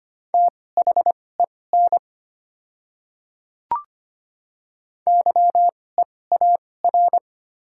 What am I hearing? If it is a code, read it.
Morse code, heard as a keyed tone, about 25 wpm, spaced as for 18 wpm, 700 Hz. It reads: THEN YEAR
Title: 2x Character Spacing - Top 300 Words - 25wpm